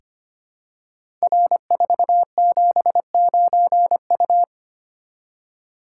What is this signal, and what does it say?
Morse code, heard as a keyed tone, about 25 wpm, 700 Hz. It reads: R479U
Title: Code Groups - 5 Letter:Numbers - 25wpm